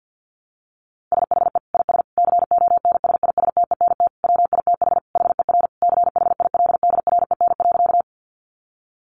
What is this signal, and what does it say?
Morse code, heard as a keyed tone, about 48 wpm, 700 Hz. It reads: SHE IS CONSISTENT WITH HER CHILDREN.
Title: Sentences from Top 1500 Words 50wpm